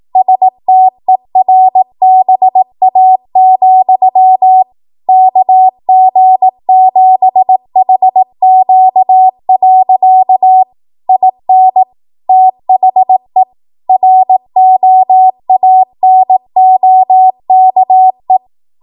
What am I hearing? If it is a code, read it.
Morse code, heard as a keyed tone, about 18 wpm, 750 Hz. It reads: STERBA, KG7HQ. IN THE ROANOKE